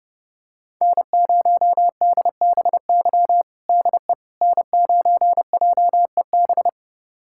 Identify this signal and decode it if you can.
Morse code, heard as a keyed tone, about 30 wpm, 700 Hz. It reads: N0DBY DE N9JEB
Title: USA Callsign DE Callsign 30wpm